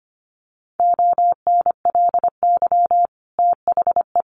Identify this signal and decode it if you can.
Morse code, heard as a keyed tone, about 25 wpm, 700 Hz. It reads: ONLY THE
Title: Sentences from Top 800 Words 25wpm